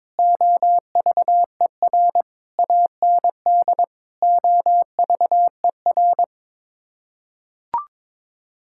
Morse code, heard as a keyed tone, about 22 wpm, 700 Hz. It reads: OVER AND OVER
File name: practice / 22wpm / Sets of 6 Words - 22wpm